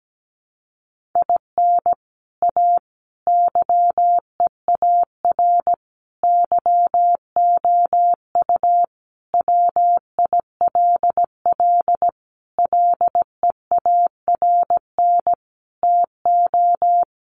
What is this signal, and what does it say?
Morse code, heard as a keyed tone, about 17 wpm, 700 Hz. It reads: IN A YEAR YOU WILL LEARN TO
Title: Sentences from Top 600 Words 17wpm